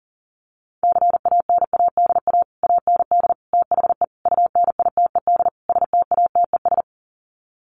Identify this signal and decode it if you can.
Morse code, heard as a keyed tone, about 40 wpm, 700 Hz. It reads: CANADA AND THE UNITED STATES
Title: Sets of 5 Words Encore 40wpm